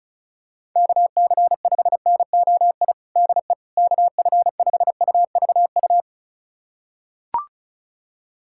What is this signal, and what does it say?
Morse code, heard as a keyed tone, about 34 wpm, 700 Hz. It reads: KC5NOI DE KF5UVU
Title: USA Callsign DE Callsign - 6 Characters - 35wpm